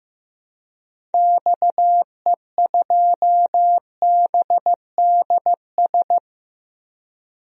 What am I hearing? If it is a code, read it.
Morse code, heard as a keyed tone, about 15 wpm, 700 Hz. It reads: XE2BDS
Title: DXCC-100 Call Signs - 6 Characters - Repeated 3x - 15wpm